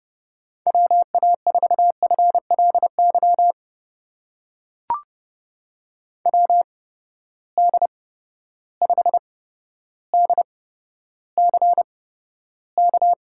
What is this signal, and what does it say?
Morse code, heard as a keyed tone, about 30 wpm, spaced as for 13 wpm, 700 Hz. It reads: WA4FLY WD5DCK
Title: USA Call Signs - 6 Characters - 8x Character Spacing - 30wpm